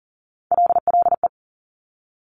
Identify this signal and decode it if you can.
Morse code, heard as a keyed tone, about 40 wpm, 700 Hz. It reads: LLE
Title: Top Three Letter Combinations in Words 40wpm